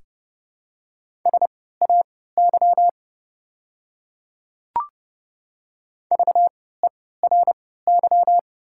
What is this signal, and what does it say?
Morse code, heard as a keyed tone, about 29 wpm, spaced as for 17 wpm, 690 Hz. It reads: SAY VERY
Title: 3x Character Spacing - Top 200 Words - 30wpm